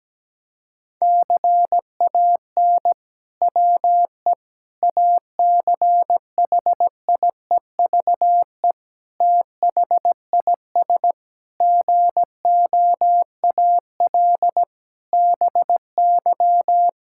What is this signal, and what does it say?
Morse code, heard as a keyed tone, about 17 wpm, 700 Hz. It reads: CAN WE ACHIEVE THIS GOAL BY
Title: Sentences from Top 1000 Words - Review Part 5 - 17wpm